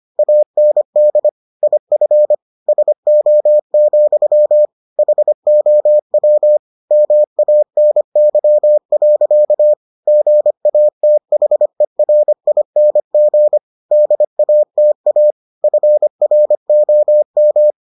Morse code, heard as a keyed tone, about 25 wpm, 600 Hz. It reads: AND IF SO, HOW MANY. GATHERING DATA FROM